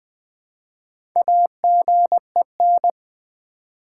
Morse code, heard as a keyed tone, about 20 wpm, 700 Hz. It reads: AGEN
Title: Build A Word - 1 Letter At A Time - 5-Letter Words - 20wpm